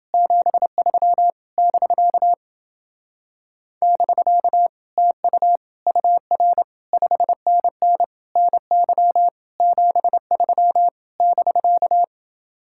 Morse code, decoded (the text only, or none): 73 <BK> <BK> TU UR 5NN NY 73 <BK>